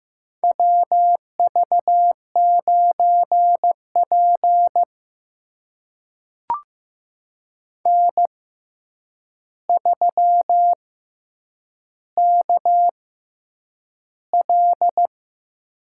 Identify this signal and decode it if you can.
Morse code, heard as a keyed tone, about 15 wpm, spaced as for 8 wpm, 700 Hz. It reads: WV9P N3KL
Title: USA Call Signs - 4 Characters - 6x Character Spacing - 15wpm